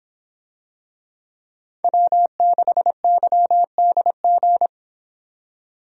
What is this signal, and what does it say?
Morse code, heard as a keyed tone, about 26 wpm, 700 Hz. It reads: W6YDG